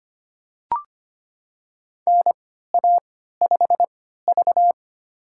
Morse code, heard as a keyed tone, about 25 wpm, spaced as for 14 wpm, 700 Hz. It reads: NA5V
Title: USA Call Signs - 4 Characters - 3x Character Spacing - 25wpm